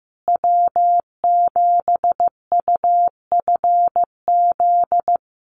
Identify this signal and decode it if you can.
Morse code, heard as a keyed tone, about 15 wpm, 700 Hz. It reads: W7UFZ
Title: USA Call Signs Repeated 3x 15wpm